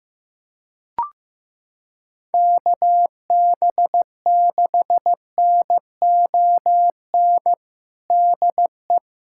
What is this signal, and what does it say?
Morse code, heard as a keyed tone, about 15 wpm, 700 Hz. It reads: KB6NON DE
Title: USA Callsign DE Callsign 15wpm